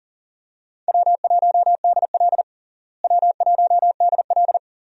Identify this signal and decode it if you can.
Morse code, heard as a keyed tone, about 37 wpm, 700 Hz. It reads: W1DL W1DL
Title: USA Call Signs - 4 Characters - Repeated 2x - 40wpm